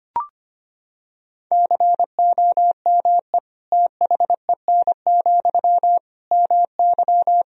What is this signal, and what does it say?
Morse code, heard as a keyed tone, about 25 wpm, 700 Hz. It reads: COME THEN, MY
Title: Sentences from Top 1000 Words - Review Part 2 - 25wpm